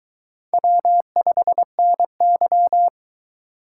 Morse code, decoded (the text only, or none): W5NY